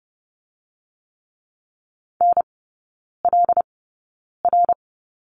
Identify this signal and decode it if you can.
Morse code, heard as a keyed tone, about 30 wpm, spaced as for 9 wpm, 700 Hz. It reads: NLR
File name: Lesson 16e - Letter c - Three Characters 30wpm